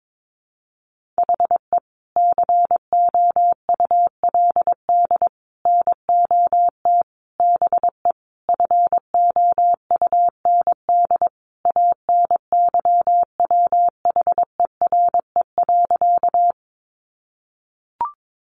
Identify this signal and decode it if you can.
Morse code, heard as a keyed tone, about 22 wpm, 700 Hz. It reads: HE COULD NOT BE FOUND ANYWHERE.
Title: Sentences from Top 1400 Words 22wpm